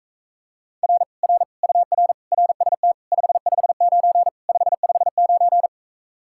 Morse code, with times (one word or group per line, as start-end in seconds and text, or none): 0.8-1.0 R
1.2-1.4 R
1.6-2.1 UR
2.3-2.9 RST
3.1-4.3 559
4.5-5.7 559